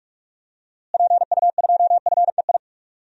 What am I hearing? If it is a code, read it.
Morse code, heard as a keyed tone, about 45 wpm, 700 Hz. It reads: PU2FEI
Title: DXCC-100 Call Signs - 6 Characters - Repeated 3x - 45wpm